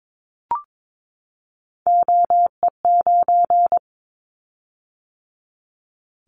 Morse code, heard as a keyed tone, about 22 wpm, 700 Hz. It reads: OE9
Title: Callsign Prefixes for Top 100 DXCC - 22wpm